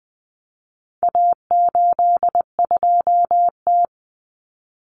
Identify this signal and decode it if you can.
Morse code, heard as a keyed tone, about 20 wpm, 700 Hz. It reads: A82T
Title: Common Cut Numbers - 4-Digit 20wpm